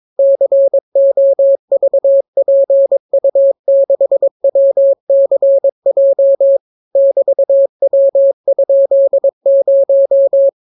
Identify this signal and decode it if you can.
Morse code, heard as a keyed tone, about 22 wpm, 550 Hz. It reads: COVPU6WCJ <BT>W?0